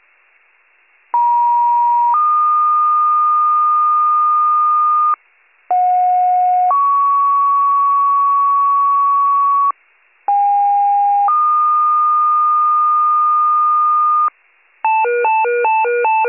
Two-tone page, heard as a one-second tone, then a three-second tone, about 950 then 1220 Hz, about 730 then 1120 Hz, about 810 then 1220 Hz. It is heard from 1.1 to 14.3 s.